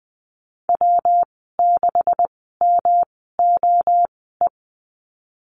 Morse code, heard as a keyed tone, about 19 wpm, spaced as for 14 wpm, 700 Hz. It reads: W6MOE
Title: 154 Lesson 25g - Letter p - Call Signs 20wpm